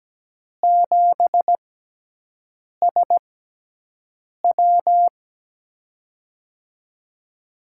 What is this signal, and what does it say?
Morse code, heard as a keyed tone, about 17 wpm, spaced as for 6 wpm, 700 Hz. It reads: 7SW
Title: USA Call Signs - 5 Characters - 6x Character Spacing - 17wpm